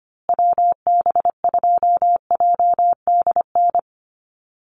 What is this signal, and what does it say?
Morse code, heard as a keyed tone, about 25 wpm, 700 Hz. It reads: WB2JDN
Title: USA Call Signs Repeated 2x 25wpm